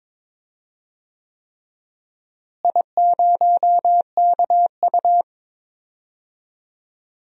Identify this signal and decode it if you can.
Morse code, heard as a keyed tone, about 21 wpm, 700 Hz. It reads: I0KU
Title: DXCC-100 Call Signs - 4 Characters - Repeated 3x - 22wpm